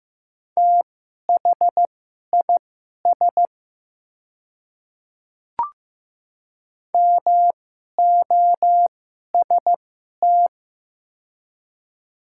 Morse code, heard as a keyed tone, about 15 wpm, spaced as for 11 wpm, 700 Hz. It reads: THIS MOST
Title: 2x Character Spacing - Top 100 Words - 15wpm